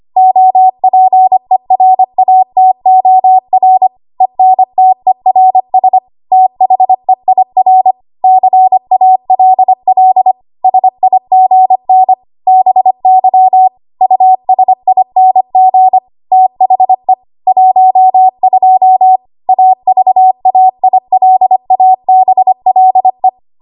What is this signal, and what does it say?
Morse code, heard as a keyed tone, about 25 wpm, 750 Hz. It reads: OPERATOR ENTERS THEIR CALL SIGN BY USING THE 12 AVAILABLE
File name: Morse / 25WPM